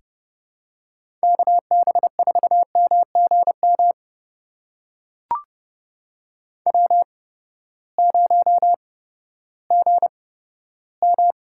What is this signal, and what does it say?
Morse code, heard as a keyed tone, about 30 wpm, spaced as for 15 wpm, 700 Hz. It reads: KB4MGM W0GM
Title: USA Call Signs - 8x Character Spacing - 30wpm